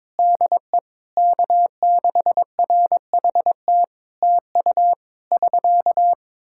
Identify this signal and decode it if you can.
Morse code, heard as a keyed tone, about 22 wpm, 700 Hz. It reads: DE K6RHT TU <SK>
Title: Practice QSO - Exchange 4 - Ending - 22wpm